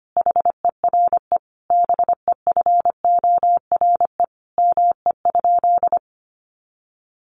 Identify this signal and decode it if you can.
Morse code, heard as a keyed tone, about 25 wpm, 700 Hz. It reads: HERE BEFORE ME?
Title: Sentences from Top 200 Words 25wpm